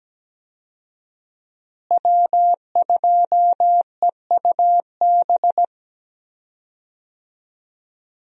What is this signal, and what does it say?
Morse code, heard as a keyed tone, about 17 wpm, 700 Hz. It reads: W2EUB
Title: USA Call Signs - 5 Characters - Repeated 2x - 17wpm